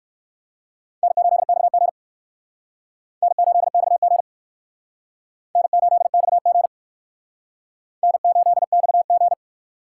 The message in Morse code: N8XG N8XG N8XG N8XG

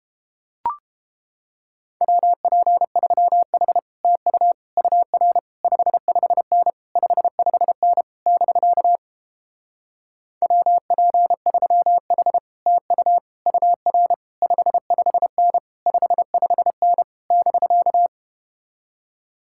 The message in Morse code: WP3H TU UR 55N 55N <BK> WP3H TU UR 55N 55N <BK>

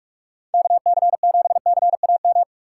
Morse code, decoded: KC7CAM

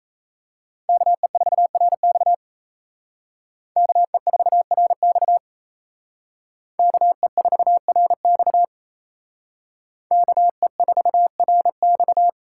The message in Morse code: KE4RX KE4RX KE4RX KE4RX

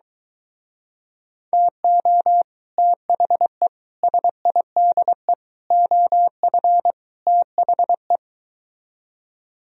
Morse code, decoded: TO THE SIDE OF THE

